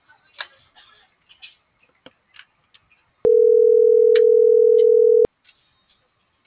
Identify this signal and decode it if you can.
Ringback tone (North America), 440 + 480 Hz.